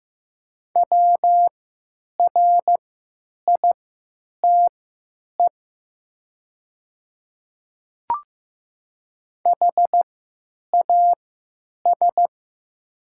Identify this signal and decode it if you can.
Morse code, heard as a keyed tone, about 15 wpm, spaced as for 9 wpm, 700 Hz. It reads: WRITE HAS
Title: 3x Character Spacing - Top 100 Words - 15wpm